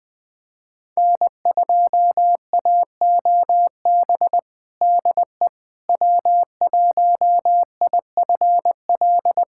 Morse code, heard as a keyed tone, about 20 wpm, 700 Hz. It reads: N2AOB DE W1IFL
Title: USA Callsign DE Callsign - 5 Characters - 20wpm